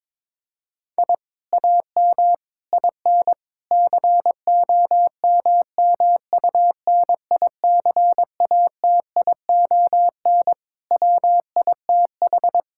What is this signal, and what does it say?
Morse code, heard as a keyed tone, about 22 wpm, 700 Hz. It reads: I AM IN COMMUNICATION WITH
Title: Sentences from Top 1000 Words - Review Part 6 - 22wpm